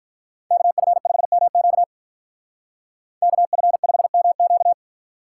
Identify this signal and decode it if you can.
Morse code, heard as a keyed tone, about 51 wpm, 700 Hz. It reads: KF5MQ KF5MQ